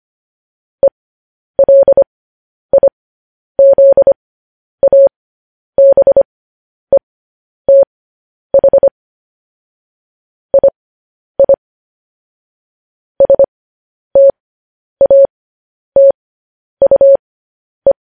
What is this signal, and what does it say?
Morse code, heard as a keyed tone, about 25 wpm, spaced as for 10 wpm, 550 Hz. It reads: ELIZABETH II STATUE
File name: podcast / News headlines-1x-25x10